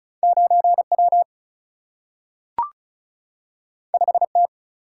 Morse code, heard as a keyed tone, about 36 wpm, 700 Hz. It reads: 9W 5T
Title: Single-Letter Single-Number 35wpm